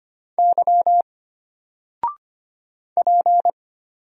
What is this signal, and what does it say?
Morse code, heard as a keyed tone, about 25 wpm, 700 Hz. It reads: Y P
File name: Single-Letter-Number-25wpm